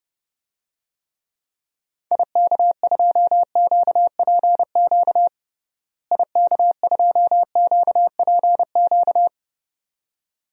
Morse code, heard as a keyed tone, about 30 wpm, 700 Hz. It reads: IK2QPQ IK2QPQ
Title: DXCC-100 Call Signs - 6 Characters - Repeated 2x - 30wpm